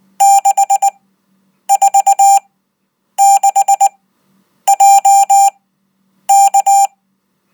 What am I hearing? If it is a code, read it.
Morse code, heard as a keyed tone, about 19 wpm, spaced as for 8 wpm, 790 Hz. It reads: 646JK